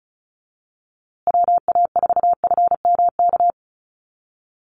Morse code, heard as a keyed tone, about 34 wpm, 700 Hz. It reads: WA4FMK
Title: USA Call Signs Repeated 3x 35wpm